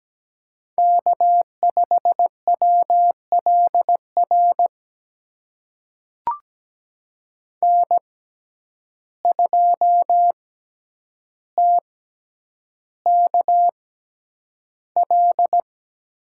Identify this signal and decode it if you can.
Morse code, heard as a keyed tone, about 17 wpm, spaced as for 9 wpm, 700 Hz. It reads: K5WLR N2TKL
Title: USA Call Signs - 5 Characters - 6x Character Spacing - 17wpm